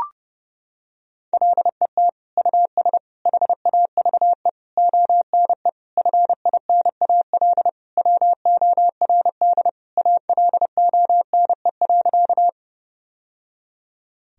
Morse code, heard as a keyed tone, about 30 wpm, 700 Hz. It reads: LET US HAVE ONE FINAL WORD ALONE.
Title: Sentences from Top 1000 Words - Review Part 2 - 30wpm